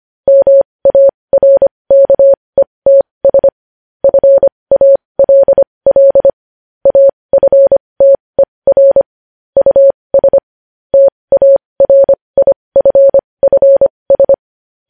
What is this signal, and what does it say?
Morse code, heard as a keyed tone, about 25 wpm, spaced as for 20 wpm, 550 Hz. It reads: MARKETS FALL AFTER US TARIFFS
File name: podcast / News headlines-1x-25x20